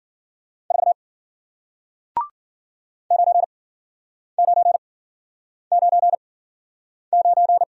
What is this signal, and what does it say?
Morse code, heard as a keyed tone, about 53 wpm, spaced as for 9 wpm, 700 Hz. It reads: 4 9999